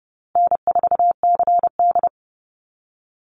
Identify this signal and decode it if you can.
Morse code, heard as a keyed tone, about 30 wpm, 700 Hz. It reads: N4CD
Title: Easy Call Signs Repeated 3x - 30wpm